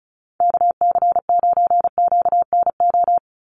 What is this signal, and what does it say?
Morse code, heard as a keyed tone, about 34 wpm, 700 Hz. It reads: KC9QNO
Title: USA Call Signs 35wpm